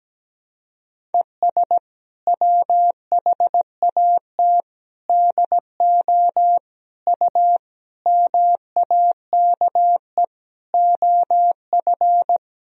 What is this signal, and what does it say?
Morse code, heard as a keyed tone, about 17 wpm, 700 Hz. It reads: ES WHAT DO U MAKE OF
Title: Sentences from Top 100 Words and Abbreviations 17wpm